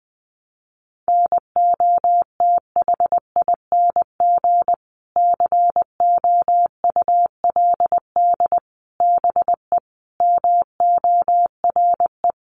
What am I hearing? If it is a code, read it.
Morse code, heard as a keyed tone, about 20 wpm, 700 Hz. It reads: NOTHING COULD BE MORE
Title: Sentences from Top 900 Words 20wpm